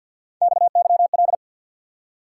Morse code, heard as a keyed tone, about 48 wpm, 700 Hz. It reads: XYL
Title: Top 180 QSO Elements 50wpm